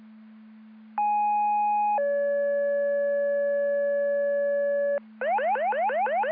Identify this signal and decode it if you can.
Two-tone page, heard as a one-second tone, then a three-second tone, about 830 then 570 Hz, with noise. It is heard from 1.0 to 5.0 s.